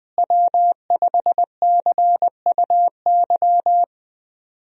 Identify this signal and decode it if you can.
Morse code, heard as a keyed tone, about 20 wpm, 700 Hz. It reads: W5CUY
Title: USA Call Signs - 5 Characters - Repeated 2x - 20wpm